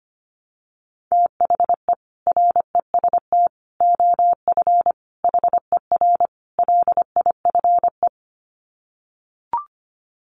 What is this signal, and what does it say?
Morse code, heard as a keyed tone, about 25 wpm, 700 Hz. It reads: THE REST OF HER LIFE
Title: Sets of 5 Words Encore 25wpm